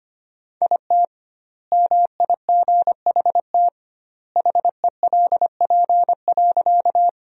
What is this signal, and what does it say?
Morse code, heard as a keyed tone, about 25 wpm, 700 Hz. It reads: IT MIGHT HELP.
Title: 2x Word-Spacing - Sentences from Top 300 Words - 25wpm